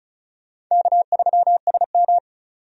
Morse code, heard as a keyed tone, about 34 wpm, 700 Hz. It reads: K3SM
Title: USA Call Signs - 4 Characters - 35wpm